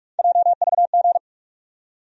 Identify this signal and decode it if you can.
Morse code, heard as a keyed tone, about 46 wpm, 700 Hz. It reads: JVG
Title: USA Call Sign Postfixes - 3 Letters 45wpm